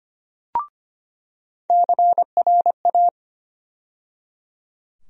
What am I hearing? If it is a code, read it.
Morse code, heard as a keyed tone, about 25 wpm, 700 Hz. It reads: CRA
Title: USA Call Sign Postfixes - 3 Letters 25wpm